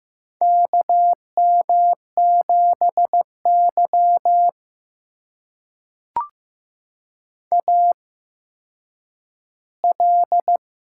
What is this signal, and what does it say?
Morse code, heard as a keyed tone, about 15 wpm, spaced as for 9 wpm, 700 Hz. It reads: KM7Y AL
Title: USA Call Signs - 4 Characters - 8x Character Spacing - 15wpm